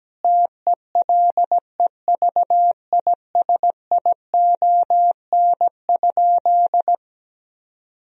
Morse code, heard as a keyed tone, about 17 wpm, 700 Hz. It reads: TELEVISION?
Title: Sentences from Top 1000 Words - Review Part 4 - 17wpm